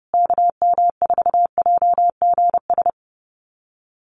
Morse code, heard as a keyed tone, about 30 wpm, 700 Hz. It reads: KM4JGS
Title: USA Call Signs Repeated 3x 30wpm